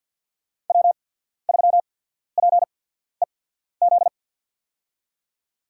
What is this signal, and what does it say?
Morse code, heard as a keyed tone, about 53 wpm, spaced as for 14 wpm, 700 Hz. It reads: W3PEZ
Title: USA Call Signs - 5 Characters - 8x Character Spacing - 50wpm